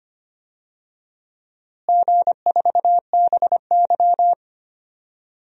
Morse code, heard as a keyed tone, about 25 wpm, 700 Hz. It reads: G4BY
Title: DXCC-100 Call Signs - 4 Characters - Repeated 3x - 25wpm